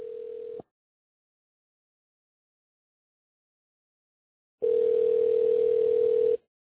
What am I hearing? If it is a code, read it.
Ringback tone (North America), 440 + 480 Hz.